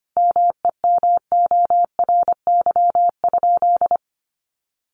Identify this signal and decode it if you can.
Morse code, heard as a keyed tone, about 25 wpm, 700 Hz. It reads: MEMORY?
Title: Sentences from Top 1300 Words 25wpm